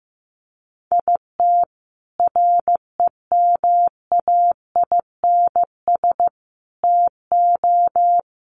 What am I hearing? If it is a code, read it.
Morse code, heard as a keyed tone, about 15 wpm, 700 Hz. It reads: IT REMAINS TO
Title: Sets of 5 Words Encore 15wpm